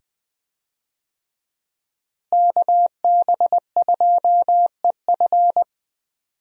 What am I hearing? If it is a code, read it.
Morse code, heard as a keyed tone, about 20 wpm, 700 Hz. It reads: KB2EF
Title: USA Call Signs - 5 Characters - Repeated 2x - 20wpm